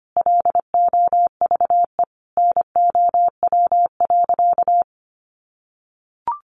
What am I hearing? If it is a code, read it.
Morse code, heard as a keyed tone, about 25 wpm, 700 Hz. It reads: LOVE NOW.
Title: Sentences from Top 1300 Words 25wpm